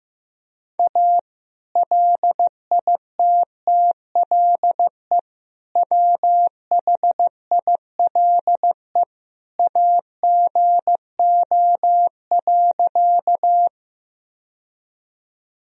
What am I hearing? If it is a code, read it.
Morse code, heard as a keyed tone, about 15 wpm, 700 Hz. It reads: A LITTLE WHILE AGO.
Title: Sentences from Top 1000 Words - Review Part 3 - 15wpm